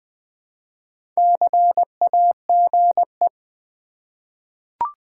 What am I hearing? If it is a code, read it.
Morse code, heard as a keyed tone, about 20 wpm, 690 Hz. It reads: CAGE
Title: Easy Words with Hints - 20wpm